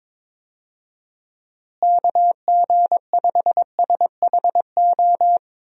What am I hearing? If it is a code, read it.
Morse code, heard as a keyed tone, about 21 wpm, 700 Hz. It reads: KG5SHO
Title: USA Call Signs - 6 Characters - Repeated 2x - 22wpm